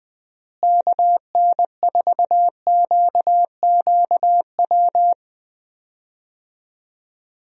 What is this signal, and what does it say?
Morse code, heard as a keyed tone, about 20 wpm, 700 Hz. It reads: KN4QQW
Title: USA Call Signs - 6 Characters - Repeated 2x - 20wpm